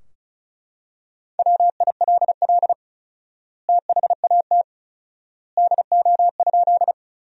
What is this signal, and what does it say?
Morse code, heard as a keyed tone, about 36 wpm, 700 Hz. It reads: WILL THAT DO?
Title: 4x Word-Spacing - Sentences from Top 100 Words - 35wpm